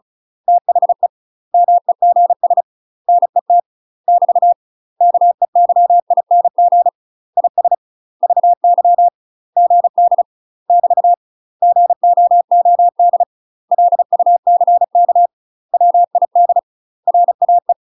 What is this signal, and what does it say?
Morse code, heard as a keyed tone, about 35 wpm, 700 Hz. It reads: THE MEGS NET <BT> KEYING IS VY GD <BT> GOOD LUCK WID RAE
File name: UK MASTER QSOs TRAINING / 35WPM / UK MASTER QSOs 0034